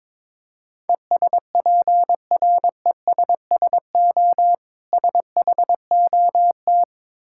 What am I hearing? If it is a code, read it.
Morse code, heard as a keyed tone, about 22 wpm, 690 Hz. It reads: ESPRESSO SHOT
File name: Difficult Words with Hints - 22wpm